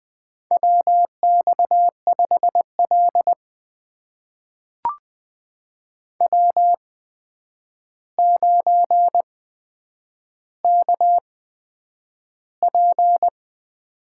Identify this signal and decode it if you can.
Morse code, heard as a keyed tone, about 20 wpm, spaced as for 9 wpm, 700 Hz. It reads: WX5L W9KP
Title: USA Call Signs - 4 Characters - 8x Character Spacing - 20wpm